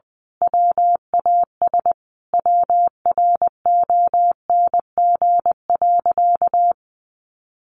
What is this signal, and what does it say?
Morse code, heard as a keyed tone, about 20 wpm, 700 Hz. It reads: WAS WRONG.
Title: Sentences from Top 1200 Words 20wpm